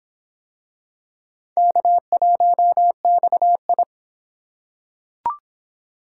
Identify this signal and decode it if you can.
Morse code, heard as a keyed tone, about 25 wpm, 700 Hz. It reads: K1XI